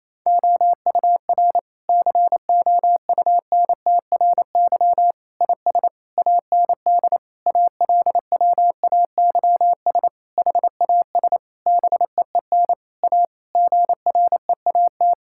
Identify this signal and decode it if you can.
Morse code, heard as a keyed tone, about 28 wpm, 700 Hz. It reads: OUR COUNTRY IS AND ALWAYS HAS BEEN A GREAT
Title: Sentences from Top 1000 Words - Review Part 6 - 28wpm